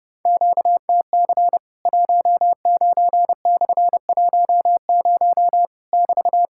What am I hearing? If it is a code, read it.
Morse code, heard as a keyed tone, about 30 wpm, 700 Hz. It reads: QTC 19/10 <BT>
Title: Worked All Europe DX Contest - QTC Exchange - 30wpm